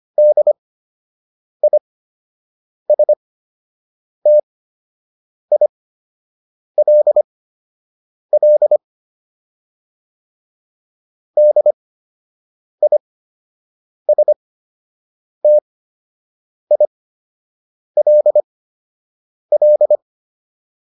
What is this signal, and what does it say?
Morse code, heard as a keyed tone, about 25 wpm, spaced as for 7 wpm, 600 Hz. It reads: DISTILL DISTILL
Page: fundamental-practice-files